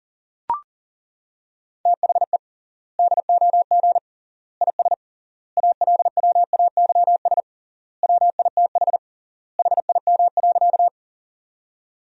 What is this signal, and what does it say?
Morse code, heard as a keyed tone, about 40 wpm, 700 Hz. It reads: THE DOG IS ALWAYS WITH HIM.
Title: 3x Word-Spacing - Sentences from Top 400 Words - 40wpm